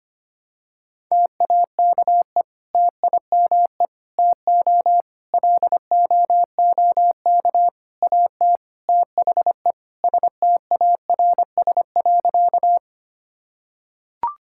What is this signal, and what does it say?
Morse code, heard as a keyed tone, about 25 wpm, 700 Hz. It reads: TAKE TIME TO LOOK AT THE STARS.
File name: practice / 25wpm / Sentences with 3-Word Phrases - 25wpm